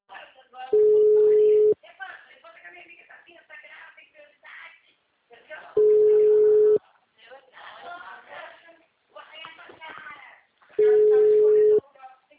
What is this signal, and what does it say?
Ringback tone (Europe, 425 Hz), with speech.